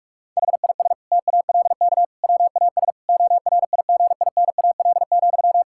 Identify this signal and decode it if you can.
Morse code, heard as a keyed tone, about 46 wpm, 700 Hz. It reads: HIS TALK WAS ORIGINAL,
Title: Sentences from Top 1000 Words - Review Part 6 - 45wpm